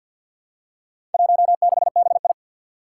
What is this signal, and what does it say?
Morse code, heard as a keyed tone, about 53 wpm, 700 Hz. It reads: 16BI
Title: Four Challenging Letter-Number Combinations 50wpm